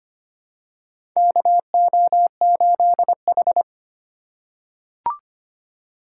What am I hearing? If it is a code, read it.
Morse code, heard as a keyed tone, about 25 wpm, 700 Hz. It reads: KO8H
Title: USA Call Signs - 4 Characters - 8x Character Spacing - 25wpm